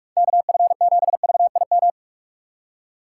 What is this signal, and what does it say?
Morse code, heard as a keyed tone, about 46 wpm, 690 Hz. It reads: KF7VIM
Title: USA Call Signs - 6 Characters - 3x Character Spacing - 45wpm